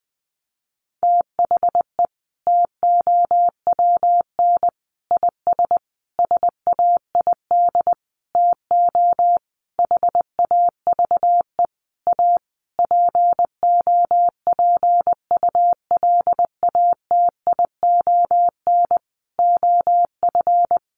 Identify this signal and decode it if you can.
Morse code, heard as a keyed tone, about 20 wpm, 700 Hz. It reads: THE TOWN IS SAID TO HAVE A POPULATION OF
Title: Sentences from Top 1300 Words 20wpm